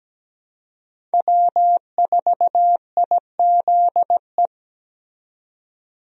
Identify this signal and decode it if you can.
Morse code, heard as a keyed tone, about 17 wpm, 700 Hz. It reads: W4IZE
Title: USA Call Signs - 5 Characters - 17wpm